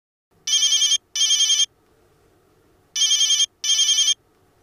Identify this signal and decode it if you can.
There is no code.